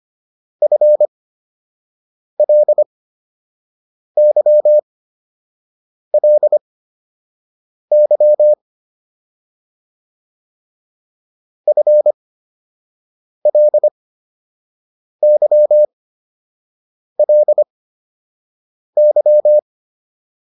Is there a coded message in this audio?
Yes: Morse code from 0.6 to 19.6 s, heard as a keyed tone, about 25 wpm, spaced as for 6 wpm, 600 Hz. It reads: FLYLY FLYLY